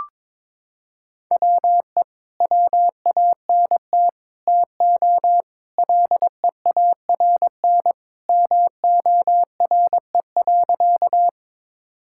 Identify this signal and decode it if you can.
Morse code, heard as a keyed tone, about 22 wpm, 700 Hz. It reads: WE WANT TO LEARN MORE.
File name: practice / 22wpm / Sentences with 3-Word Phrases - 22wpm